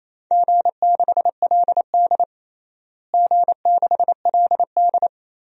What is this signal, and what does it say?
Morse code, heard as a keyed tone, about 28 wpm, 700 Hz. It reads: G6LD G6LD
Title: DXCC-100 Call Signs - 4 Characters - Repeated 2x - 28wpm